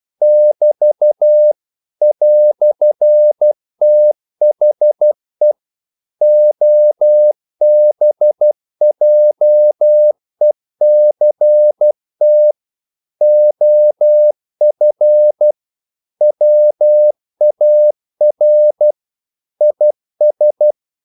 Morse code, heard as a keyed tone, about 12 wpm, 600 Hz. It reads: <BT> "THE OBJECT OF WAR IS
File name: CW / Practice / EnglishQuotations0312